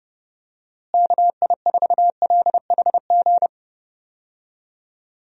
Morse code, heard as a keyed tone, about 31 wpm, 700 Hz. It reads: KI4LHG
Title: USA Call Signs - 6 Characters - Repeated 2x - 30wpm